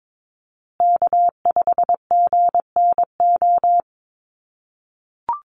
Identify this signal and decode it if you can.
Morse code, heard as a keyed tone, about 22 wpm, 700 Hz. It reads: K5GNO
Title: USA Call Signs Repeated 2x 22wpm